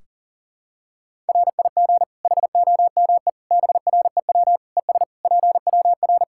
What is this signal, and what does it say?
Morse code, heard as a keyed tone, about 40 wpm, 700 Hz. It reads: RIG HOME BREW ES PWR
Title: Practice QSO - Exchange 2 - Info - 40wpm